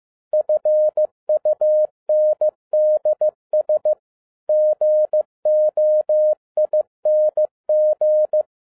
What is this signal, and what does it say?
Morse code, heard as a keyed tone, about 15 wpm, 610 Hz. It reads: FUNDS GOING
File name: 2012-01-mp3 / n13-A NEWLY DESIGNED QSL CARD-15wpm